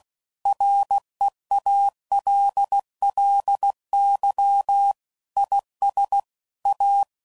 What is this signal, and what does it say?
Morse code, heard as a keyed tone, about 16 wpm, 800 Hz. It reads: REALLY IS A
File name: CW Code Groups @15WPM / Technical Subjects / Gates Intro